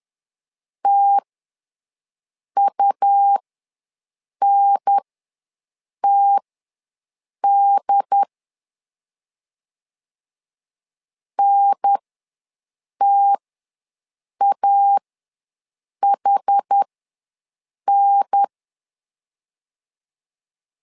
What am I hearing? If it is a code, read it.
Morse code, heard as a keyed tone, about 10 wpm, spaced as for 6 wpm, 800 Hz. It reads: TUNTD NTAHN
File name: LEZIONI / 2 UHDN / 2 UHDN ws 10wpm 2044